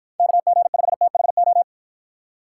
Morse code, heard as a keyed tone, about 49 wpm, 700 Hz. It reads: KG5THO